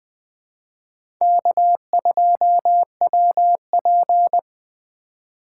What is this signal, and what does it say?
Morse code, heard as a keyed tone, about 20 wpm, 700 Hz. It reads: K2WP
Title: USA Call Signs - 4 Characters - 4x Character Spacing - 20wpm